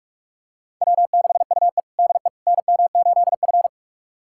Morse code, heard as a keyed tone, about 42 wpm, 700 Hz. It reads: W6UE DE NM8F
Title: USA Callsign DE Callsign - 4 Characters - 45wpm